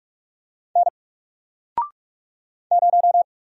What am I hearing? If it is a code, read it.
Morse code, heard as a keyed tone, about 46 wpm, 700 Hz. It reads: N 0
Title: Single Letter-Number Alt 45wpm